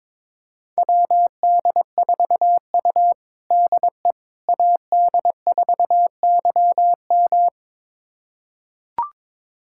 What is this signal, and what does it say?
Morse code, heard as a keyed tone, about 22 wpm, 700 Hz. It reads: WD4U DE AD4YM
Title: USA Callsign DE Callsign 22wpm